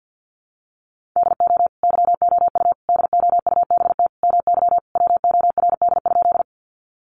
Morse code, heard as a keyed tone, about 50 wpm, 700 Hz. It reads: DO YOU DOUBT MY WORD?
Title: Sentences from Top 1500 Words 50wpm